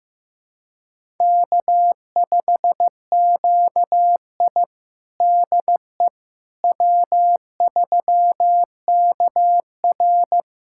Morse code, heard as a keyed tone, about 16 wpm, 700 Hz. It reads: K5QI DE W3KR
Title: USA Callsign DE Callsign - 4 Characters - 15wpm